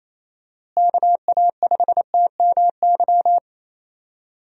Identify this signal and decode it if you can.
Morse code, heard as a keyed tone, about 28 wpm, 700 Hz. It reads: KA5TMY
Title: USA Call Signs - 6 Characters - Repeated 3x - 28wpm